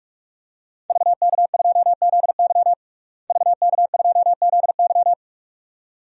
Morse code, heard as a keyed tone, about 44 wpm, 700 Hz. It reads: VK2ZY VK2ZY
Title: DXCC-100 Call Signs - 5 Characters - Repeated 2x - 45wpm